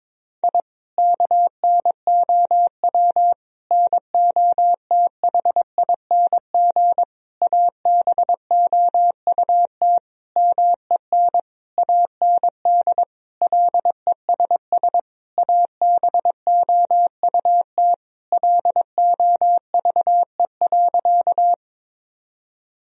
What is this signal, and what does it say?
Morse code, heard as a keyed tone, about 22 wpm, 700 Hz. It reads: I KNOW NOTHING ABOUT MEN AND LESS ABOUT LOVE.
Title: Sentences from Top 1000 Words - Review Part 2 - 22wpm